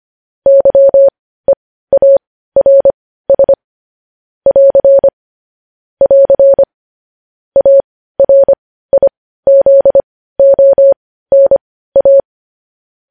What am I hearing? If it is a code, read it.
Morse code, heard as a keyed tone, about 25 wpm, spaced as for 15 wpm, 550 Hz. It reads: YEARS <AR> <AR> ARIZONA
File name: podcast / News headlines-1x-25x15